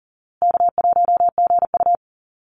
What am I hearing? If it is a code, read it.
Morse code, heard as a keyed tone, about 40 wpm, 700 Hz. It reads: K1GU
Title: Easy Call Signs Repeated 3x - 40wpm